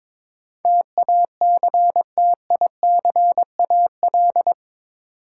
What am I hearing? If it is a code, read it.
Morse code, heard as a keyed tone, about 22 wpm, 700 Hz. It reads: TACTICAL
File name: practice / 22wpm / Eight Letter Words - 22wpm